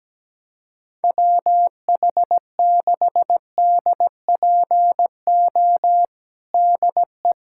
Morse code, heard as a keyed tone, about 17 wpm, 700 Hz. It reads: WH6DPO DE
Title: USA Callsign DE Callsign 17wpm